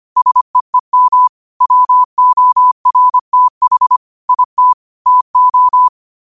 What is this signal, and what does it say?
Morse code, heard as a keyed tone, about 25 wpm, 1000 Hz. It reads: SEEM WORTH IT TO
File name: qotdfiles / 25wpm